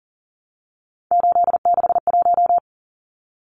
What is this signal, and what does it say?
Morse code, heard as a keyed tone, about 40 wpm, 700 Hz. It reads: 861
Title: Numbers - 3-Digit 40wpm